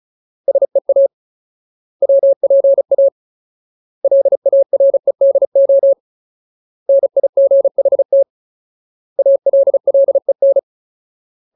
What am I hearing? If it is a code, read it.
Morse code, heard as a keyed tone, about 34 wpm, 550 Hz. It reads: SEA WPA LAREDO NIGHT ALLEN
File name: Ultimate Word List / 35WPM Ultimate